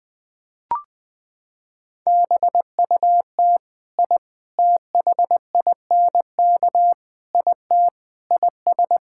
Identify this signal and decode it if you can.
Morse code, heard as a keyed tone, about 20 wpm, 700 Hz. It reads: BUT I THINK IT IS
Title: Sentences with 4-Word Phrases - 20wpm